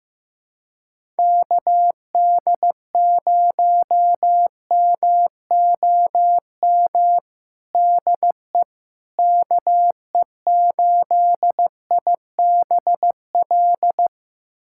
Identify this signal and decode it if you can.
Morse code, heard as a keyed tone, about 16 wpm, 700 Hz. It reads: KD0MOM DE KE8IBL
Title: USA Callsign DE Callsign - 6 Characters - 15wpm